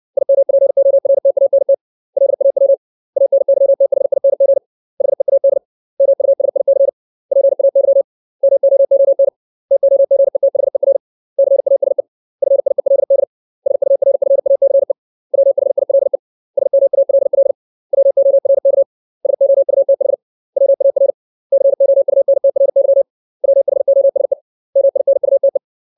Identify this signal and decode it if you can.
Morse code, heard as a keyed tone, about 60 wpm, 550 Hz. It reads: IMPORTANT LAW ANYTHING HEAD MUSIC PAY MOON TOGETHER CASE FIELD SURFACE WHILE SMALL WORK SOUTH WAR COUNTRY WHOSE MINUTE
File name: 632 words / 632 Words 60WPM